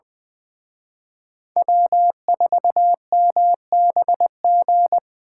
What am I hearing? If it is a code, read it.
Morse code, heard as a keyed tone, about 24 wpm, 700 Hz. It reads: W4MBG
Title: Speed Racing - USA 1x3 Call Signs 20wpm